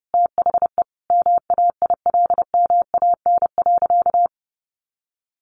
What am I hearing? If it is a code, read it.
Morse code, heard as a keyed tone, about 30 wpm, 700 Hz. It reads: THE MAILMAN.
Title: Additional Easy Phrases 30wpm